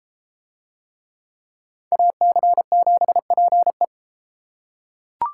Morse code, heard as a keyed tone, about 32 wpm, 700 Hz. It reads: AC7PE